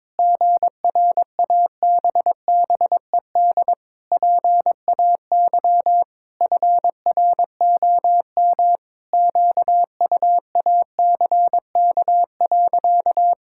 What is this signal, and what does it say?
Morse code, heard as a keyed tone, about 22 wpm, 700 Hz. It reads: GRABBED PAY FROM QUACK.
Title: Pangram Challenge v2 - 22wpm